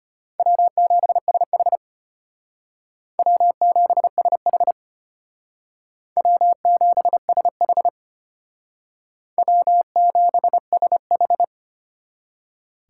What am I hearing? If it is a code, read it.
Morse code, heard as a keyed tone, about 31 wpm, 700 Hz. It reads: W7SH W7SH W7SH W7SH